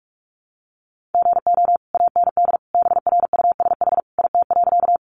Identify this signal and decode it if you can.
Morse code, heard as a keyed tone, about 44 wpm, 700 Hz. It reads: GO AND BRUSH IT.
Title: Sentences from Top 1100 Words 45wpm